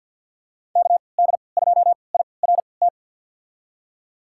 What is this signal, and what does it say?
Morse code, heard as a keyed tone, about 47 wpm, spaced as for 28 wpm, 700 Hz. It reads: KD2IRT
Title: USA Call Signs - 6 Characters - 3x Character Spacing - 50wpm